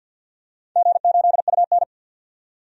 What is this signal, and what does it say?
Morse code, heard as a keyed tone, about 52 wpm, 700 Hz. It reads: G8UN
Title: DXCC-100 Call Signs - 4 Characters - 50wpm